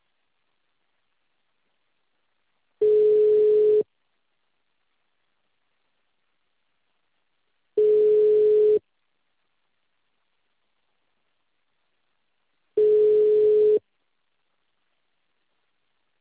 Ringback tone (Europe, 425 Hz).
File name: out- -2044- - - .2636